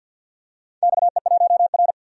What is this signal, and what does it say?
Morse code, heard as a keyed tone, about 50 wpm, 700 Hz. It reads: XE1R
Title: DXCC-100 Call Signs - 4 Characters - 50wpm